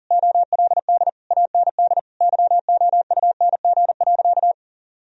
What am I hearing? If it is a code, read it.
Morse code, heard as a keyed tone, about 40 wpm, 700 Hz. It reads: OLD AND YOUNG.
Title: Sentences from Top 1000 Words - Review Part 3 - 40wpm